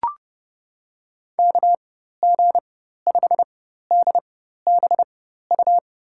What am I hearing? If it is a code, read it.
Morse code, heard as a keyed tone, about 32 wpm, spaced as for 14 wpm, 700 Hz. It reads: KG5DBU
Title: USA Call Signs - 6 Characters - 4x Character Spacing - 30wpm